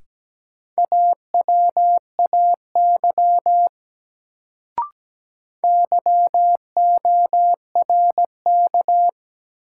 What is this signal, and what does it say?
Morse code, heard as a keyed tone, about 17 wpm, 700 Hz. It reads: AWAY YORK
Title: Sets of 2 Words - New Word Drill - 17wpm